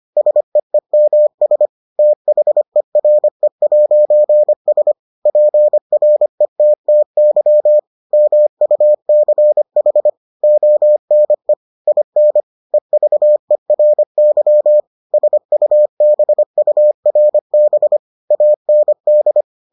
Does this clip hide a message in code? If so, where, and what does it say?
Yes: Morse code from 0.2 to 19.4 s, heard as a keyed tone, about 25 wpm, 600 Hz. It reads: SEEMS THERE'S PRETTY MUCH ONE IN EVERY SUBURB AND